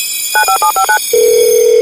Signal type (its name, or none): DTMF